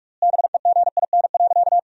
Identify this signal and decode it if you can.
Morse code, heard as a keyed tone, about 44 wpm, 700 Hz. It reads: BEGIN.
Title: 4x Word-Spacing - Sentences from Top 300 Words - 45wpm